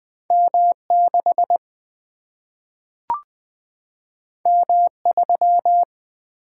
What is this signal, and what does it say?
Morse code, heard as a keyed tone, about 20 wpm, 700 Hz. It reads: M 6 M 3
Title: Single-Letter Single-Number 20wpm